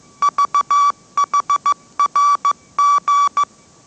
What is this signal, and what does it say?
Morse code, heard as a keyed tone, about 15 wpm, 1150 Hz. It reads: VHRG
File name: mrx / mrx065 16wpm